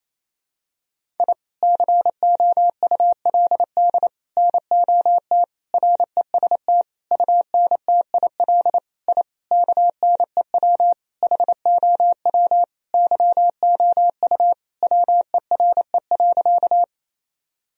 Morse code, heard as a keyed tone, about 28 wpm, 700 Hz. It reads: I COULD NOT REST UNTIL I KNEW HOW YOU WERE.
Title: Sentences from Top 1000 Words - Review Part 1 - 28wpm